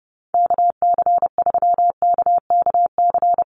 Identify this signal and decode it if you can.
Morse code, heard as a keyed tone, about 30 wpm, 700 Hz. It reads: KC3KKC
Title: USA Call Signs Repeated 3x 30wpm